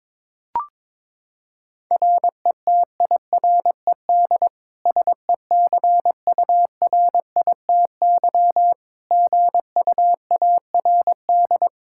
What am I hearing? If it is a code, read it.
Morse code, heard as a keyed tone, about 22 wpm, 700 Hz. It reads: RETIRED SECURITY GUARD
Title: K1USN SST QSO Practice 22wpm